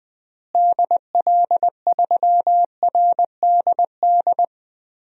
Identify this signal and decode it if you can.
Morse code, heard as a keyed tone, about 20 wpm, 700 Hz. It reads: DL3RDD
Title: DXCC-100 Call Signs - 6 Characters - Repeated 3x - 20wpm